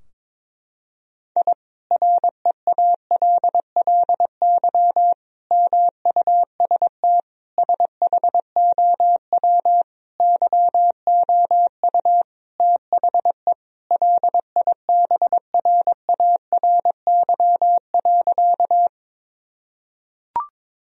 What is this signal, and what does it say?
Morse code, heard as a keyed tone, about 21 wpm, 700 Hz. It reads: I REALLY MUST SHOW YOU THE LIBRARY.
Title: Sentences from Top 1000 Words - Review Part 4 - 22wpm